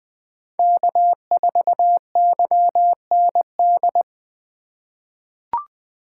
Morse code, heard as a keyed tone, about 20 wpm, 700 Hz. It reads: K4YND
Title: K1USN SST QSO Practice - 20wpm